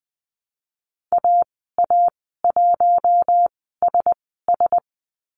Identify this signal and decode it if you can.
Morse code, heard as a keyed tone, about 20 wpm, spaced as for 15 wpm, 700 Hz. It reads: AA1SS
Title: 055 Lesson 10g - Letter r - Call Signs 20wpm